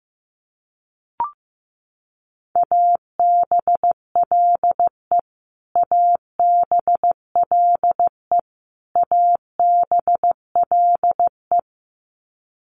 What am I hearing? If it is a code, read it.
Morse code, heard as a keyed tone, about 15 wpm, 700 Hz. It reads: ABLE ABLE ABLE
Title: Top 500 Words Repeated 15wpm